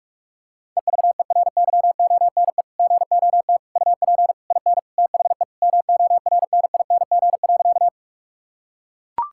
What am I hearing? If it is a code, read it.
Morse code, heard as a keyed tone, about 41 wpm, 700 Hz. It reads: EVERYONE GOT UP IN THE MORNING.
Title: Sentences from Top 1000 Words - Review Part 5 - 45wpm